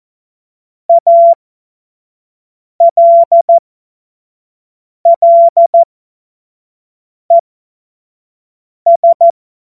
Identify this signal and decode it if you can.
Morse code, heard as a keyed tone, about 14 wpm, spaced as for 5 wpm, 680 Hz. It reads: ALLES
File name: Morsekurs / Witze / Witz 10.Oktober 2023 25 F70